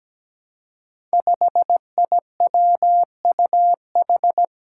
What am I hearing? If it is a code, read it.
Morse code, heard as a keyed tone, about 17 wpm, 700 Hz. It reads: 5IWUH
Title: Five Challenging Letter-Number Combinations 17wpm